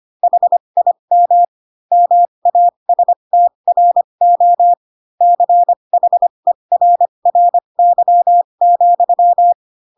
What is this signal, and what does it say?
Morse code, heard as a keyed tone, about 25 wpm, spaced as for 22 wpm, 700 Hz. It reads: HIM MASTRO CHERRY,